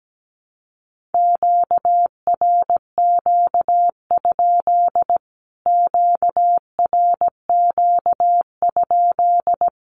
Morse code, heard as a keyed tone, about 17 wpm, 700 Hz. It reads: QRQ? QRQ?
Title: Q-Codes Repeated 3x 17wpm